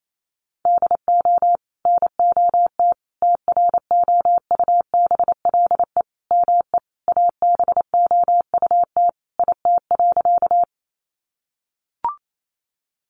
Morse code, heard as a keyed tone, about 28 wpm, 700 Hz. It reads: DO NOT TROUBLE ME ABOUT IT.
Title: Sentences from Top 1200 Words 28wpm